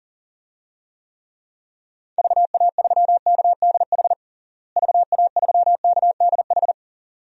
Morse code, heard as a keyed tone, about 40 wpm, 700 Hz. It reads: VA3KDH VA3KDH
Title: DXCC-100 Call Signs - 6 Characters - Repeated 2x - 40wpm